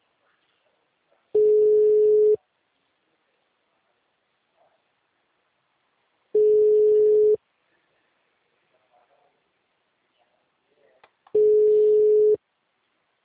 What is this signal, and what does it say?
Ringback tone (Europe, 425 Hz).
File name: out- -2035- - - .15281